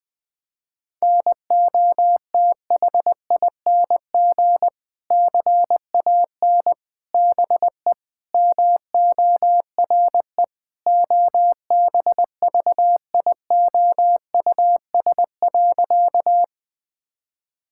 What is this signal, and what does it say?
Morse code, heard as a keyed tone, about 21 wpm, 700 Hz. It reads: NOTHING CAN BE MORE OBVIOUS.
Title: Sentences from Top 1600 Words 20wpm